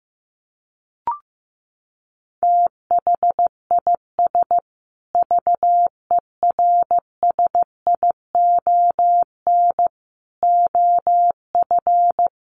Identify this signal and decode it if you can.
Morse code, heard as a keyed tone, about 15 wpm, 700 Hz. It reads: THIS VERSION OF
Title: Sentences from Top 1400 Words 15wpm